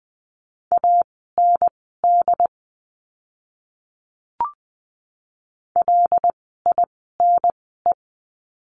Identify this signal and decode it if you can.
Morse code, heard as a keyed tone, about 20 wpm, spaced as for 15 wpm, 700 Hz. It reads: AND LINE
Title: 115 Lesson 19f - Letter w - Words 20wpm